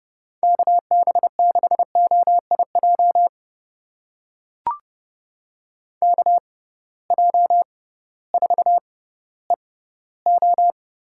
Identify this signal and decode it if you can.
Morse code, heard as a keyed tone, about 32 wpm, spaced as for 16 wpm, 700 Hz. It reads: KB6OIJ KJ4EO
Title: USA Call Signs - 6 Characters - 6x Character Spacing - 30wpm